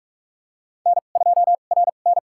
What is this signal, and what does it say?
Morse code, heard as a keyed tone, about 41 wpm, spaced as for 31 wpm, 700 Hz. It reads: N2RN
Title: USA Call Signs - 2x Character Spacing - 45wpm